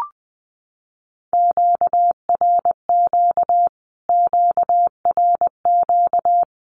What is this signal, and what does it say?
Morse code, heard as a keyed tone, about 20 wpm, 700 Hz. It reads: QRQ QRQ
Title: Q-Codes Repeated 3x 20wpm